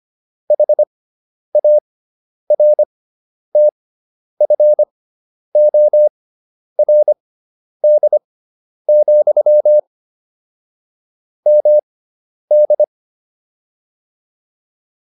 Morse code, heard as a keyed tone, about 25 wpm, spaced as for 10 wpm, 600 Hz. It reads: HARTFORD, MD
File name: QSO401 10